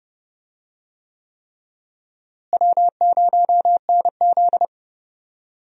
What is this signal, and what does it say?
Morse code, heard as a keyed tone, about 31 wpm, 700 Hz. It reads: W0NZ
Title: USA Call Signs - 4 Characters - Repeated 3x - 30wpm